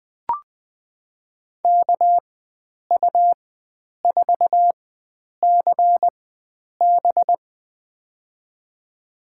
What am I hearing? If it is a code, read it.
Morse code, heard as a keyed tone, about 20 wpm, spaced as for 9 wpm, 700 Hz. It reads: KU4CB
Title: USA Call Signs - 5 Characters - 4x Character Spacing - 20wpm